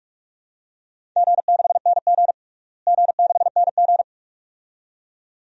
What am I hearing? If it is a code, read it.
Morse code, heard as a keyed tone, about 46 wpm, 700 Hz. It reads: G6NG G6NG
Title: DXCC-100 Call Signs - 4 Characters - Repeated 2x - 45wpm